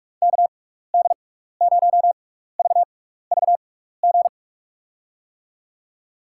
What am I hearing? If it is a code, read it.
Morse code, heard as a keyed tone, about 46 wpm, spaced as for 15 wpm, 700 Hz. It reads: KD0VVG
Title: USA Call Signs - 6 Characters - 6x Character Spacing - 45wpm